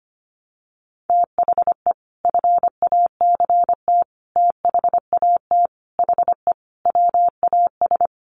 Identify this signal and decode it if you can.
Morse code, heard as a keyed tone, about 25 wpm, 700 Hz. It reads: THE FACT THAT HE WAS
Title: Sets of 5 Words 25wpm